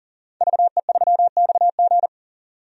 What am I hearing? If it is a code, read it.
Morse code, heard as a keyed tone, about 39 wpm, 700 Hz. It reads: VE3XG
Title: DXCC-100 Call Signs - 5 Characters - 40wpm